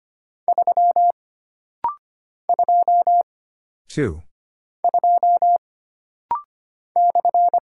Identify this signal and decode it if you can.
Morse code, heard as a keyed tone, about 25 wpm, 700 Hz. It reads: 3 2 2 /
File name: Single Character (Letter:Number:Punctuation:Prosign) - Rapid-Fire - 25wpm